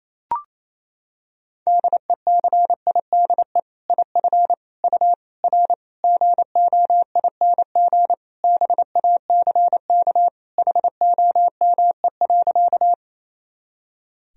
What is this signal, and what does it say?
Morse code, heard as a keyed tone, about 28 wpm, 700 Hz. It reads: DECIDE IF U R GOING BACK HOME.
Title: Sentences from Top 500 Words and Abbreviations 28wpm